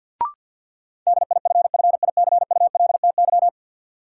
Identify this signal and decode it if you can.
Morse code, heard as a keyed tone, about 49 wpm, 700 Hz. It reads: DIFFICULTY
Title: Sentences from Top 1500 Words - New Word Drill Alt - 50wpm